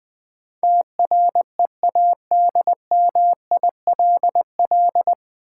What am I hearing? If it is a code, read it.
Morse code, heard as a keyed tone, about 20 wpm, 700 Hz. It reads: TREADMILL
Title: Easy Words with Hints - 20wpm